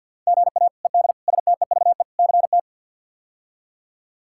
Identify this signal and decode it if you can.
Morse code, heard as a keyed tone, about 46 wpm, 700 Hz. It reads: GA ED STEVE CT
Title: K1USN SST QSO Practice - 50wpm